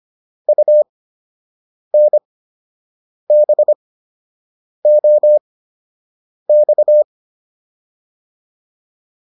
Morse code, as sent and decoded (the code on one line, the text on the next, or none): ..- -. -... --- -..-
UNBOX